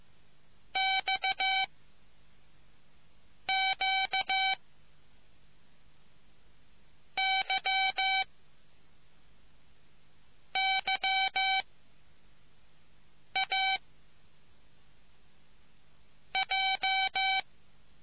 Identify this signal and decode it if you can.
Morse code, heard as a keyed tone, about 15 wpm, 770 Hz. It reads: X Q Y Y A J